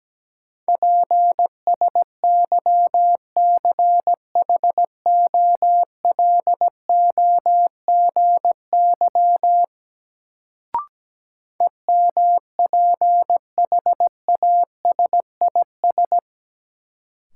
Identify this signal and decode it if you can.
Morse code, heard as a keyed tone, about 17 wpm, 700 Hz. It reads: PSYCHOLOGY EMPHASIS
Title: Sentences from Top 1400 Words - New Word Drill - 17wpm